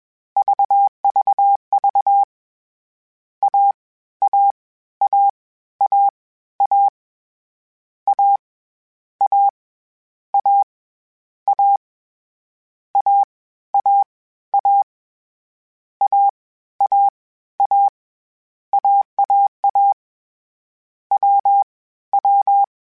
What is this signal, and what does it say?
Morse code, heard as a keyed tone, about 21 wpm, 800 Hz. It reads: VVV A A A A A A A A A A A A A A A AAA W W
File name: CW Code Groups-mp3s / Sets mp3s / AUV sp @20